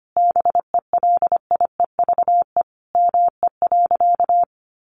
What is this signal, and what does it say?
Morse code, heard as a keyed tone, about 25 wpm, 700 Hz. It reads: BELIEVE ME.
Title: Sentences from Top 1000 Words 25wpm